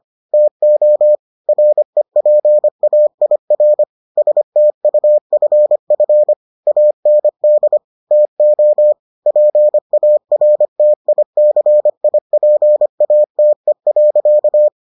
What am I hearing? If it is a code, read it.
Morse code, heard as a keyed tone, about 25 wpm, 600 Hz. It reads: TO REPAIR STUFF AND TO PARTICIPATE.